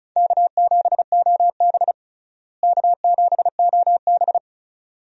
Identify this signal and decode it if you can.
Morse code, heard as a keyed tone, about 36 wpm, 700 Hz. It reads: K7OB K7OB
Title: USA Call Signs - 4 Characters - Repeated 2x - 35wpm